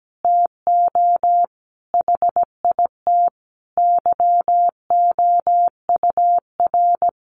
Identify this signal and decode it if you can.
Morse code, heard as a keyed tone, about 17 wpm, 700 Hz. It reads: TO HIT YOUR
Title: Sentences from Top 1200 Words 17wpm